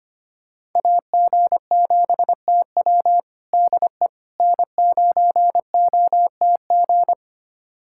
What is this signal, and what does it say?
Morse code, heard as a keyed tone, about 25 wpm, 700 Hz. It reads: AG7TW DE N9OTG
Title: USA Callsign DE Callsign - 5 Characters - 25wpm